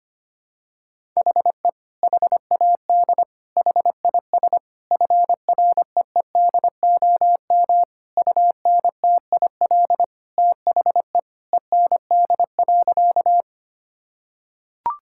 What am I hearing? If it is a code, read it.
Morse code, heard as a keyed tone, about 25 wpm, 700 Hz. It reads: HE HAD HIS FREEDOM UNTIL THE END.
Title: Sentences from Top 1000 Words - Review Part 5 - 25wpm